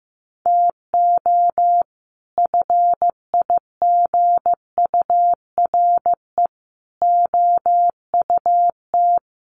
Morse code, heard as a keyed tone, about 14 wpm, 700 Hz. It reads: TO FIGURE OUT
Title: Sets of 5 Words 15wpm